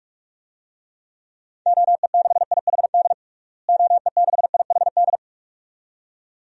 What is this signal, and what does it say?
Morse code, heard as a keyed tone, about 46 wpm, 700 Hz. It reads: OE6IHD OE6IHD
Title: DXCC-100 Call Signs - 6 Characters - Repeated 2x - 45wpm